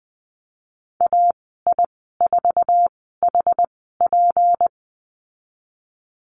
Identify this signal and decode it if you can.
Morse code, heard as a keyed tone, about 20 wpm, spaced as for 14 wpm, 700 Hz. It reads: AI4HP
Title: 161 Lesson 26g - Letter g - Call Signs 20wpm